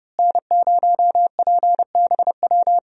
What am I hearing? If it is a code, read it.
Morse code, heard as a keyed tone, about 30 wpm, 700 Hz. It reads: N0PBW
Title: USA Call Signs - 5 Characters - Repeated 2x - 30wpm